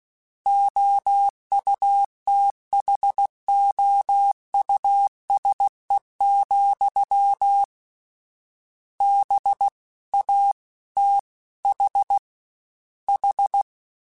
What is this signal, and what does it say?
Morse code, heard as a keyed tone, about 16 wpm, 800 Hz. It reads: OUTHOUSE, B A T H H
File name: General / General-mp3s / Houses @15